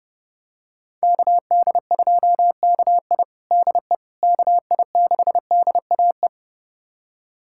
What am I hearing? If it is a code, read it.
Morse code, heard as a keyed tone, about 30 wpm, 700 Hz. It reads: KD2KI DE KI6DAE
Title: USA Callsign DE Callsign 30wpm